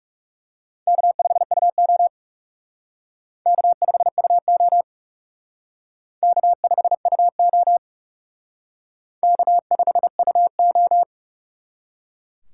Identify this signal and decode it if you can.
Morse code, heard as a keyed tone, about 38 wpm, 700 Hz. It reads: K5UO K5UO K5UO K5UO